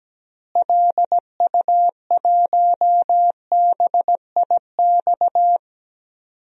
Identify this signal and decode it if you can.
Morse code, heard as a keyed tone, about 17 wpm, 700 Hz. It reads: LU1BIX
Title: DXCC-100 Call Signs - 6 Characters - Repeated 3x - 17wpm